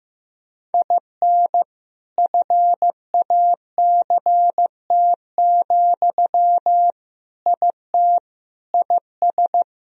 Morse code, heard as a keyed tone, about 15 wpm, 700 Hz. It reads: IN FACT, IT IS
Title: Sentences from Top 1000 Words - Review Part 3 - 15wpm